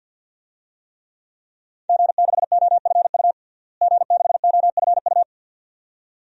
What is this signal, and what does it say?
Morse code, heard as a keyed tone, about 53 wpm, 700 Hz. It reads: G6OFU G6OFU
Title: DXCC-100 Call Signs - 5 Characters - Repeated 2x - 50wpm